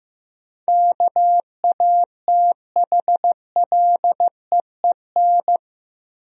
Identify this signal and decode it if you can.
Morse code, heard as a keyed tone, about 16 wpm, 700 Hz. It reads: KATHLEEN
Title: USA Common Names - 15wpm